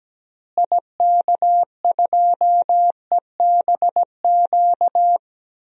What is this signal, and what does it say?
Morse code, heard as a keyed tone, about 17 wpm, 700 Hz. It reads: IK2EBQ
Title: DXCC-100 Call Signs - 6 Characters - Repeated 3x - 17wpm